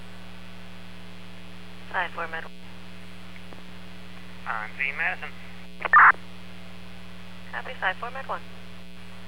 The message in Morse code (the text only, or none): none